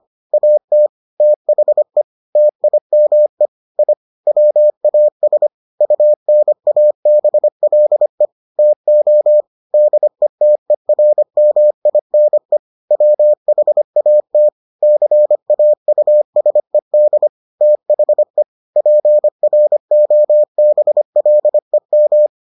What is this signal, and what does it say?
Morse code, heard as a keyed tone, about 25 wpm, 600 Hz. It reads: AT THE TIME I WAS UNABLE TO DETERMINE WHAT CAUSED THE PROBLEM